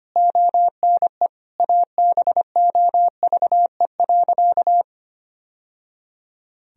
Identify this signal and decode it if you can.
Morse code, heard as a keyed tone, about 25 wpm, 700 Hz. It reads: ONE ABOVE.
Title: Sentences from Top 1000 Words - Review Part 2 - 25wpm